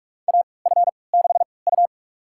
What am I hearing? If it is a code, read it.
Morse code, heard as a keyed tone, about 46 wpm, spaced as for 25 wpm, 700 Hz. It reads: AF6U
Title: USA Call Signs - 4 Characters - 3x Character Spacing - 45wpm